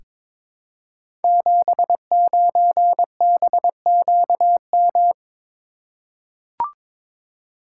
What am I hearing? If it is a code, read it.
Morse code, heard as a keyed tone, about 22 wpm, 700 Hz. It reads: 79BQM
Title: Code Groups - 5 Letter:Numbers - 22wpm